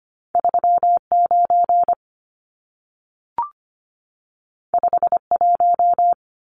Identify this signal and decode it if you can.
Morse code, heard as a keyed tone, about 25 wpm, 700 Hz. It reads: 39 51
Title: Numbers - 2-Digit 25wpm